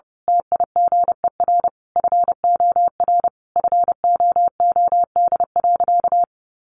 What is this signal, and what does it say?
Morse code, heard as a keyed tone, about 30 wpm, 700 Hz. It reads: TIGER FOR FOOD.
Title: A Sound Of Thunder 30wpm